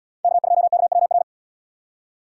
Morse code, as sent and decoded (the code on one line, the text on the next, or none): -.- ..--- -.- .-- .-.
K2KWR